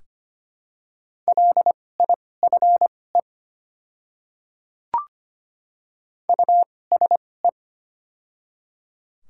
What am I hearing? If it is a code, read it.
Morse code, heard as a keyed tone, about 25 wpm, spaced as for 18 wpm, 690 Hz. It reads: LIFE USE
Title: 2x Character Spacing - Top 300 Words - 25wpm